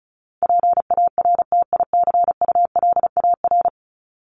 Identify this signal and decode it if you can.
Morse code, heard as a keyed tone, about 34 wpm, 700 Hz. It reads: PARTICULAR
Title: LY Suffix - Top 100 Words with Base Words 35wpm